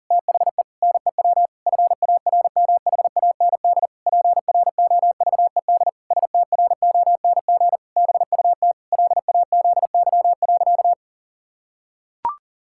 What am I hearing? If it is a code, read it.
Morse code, heard as a keyed tone, about 40 wpm, 700 Hz. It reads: THE NEW FARMHAND PROVED STRONG BUT LAZY.
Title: Pangram Challenge v2 - 40wpm